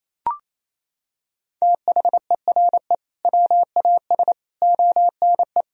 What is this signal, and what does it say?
Morse code, heard as a keyed tone, about 28 wpm, 700 Hz. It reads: THERE WAS ONE
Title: Sentences from Top 1000 Words - Review Part 3 - 28wpm